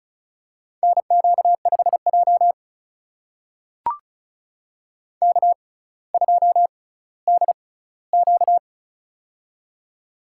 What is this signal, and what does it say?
Morse code, heard as a keyed tone, about 34 wpm, spaced as for 18 wpm, 700 Hz. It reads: NQ5J K2DQ
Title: USA Call Signs - 4 Characters - 6x Character Spacing - 35wpm